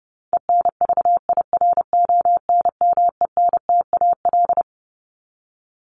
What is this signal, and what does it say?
Morse code, heard as a keyed tone, about 30 wpm, 700 Hz. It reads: ENVIRONMENTAL
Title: Top 1001-1250 Words 30wpm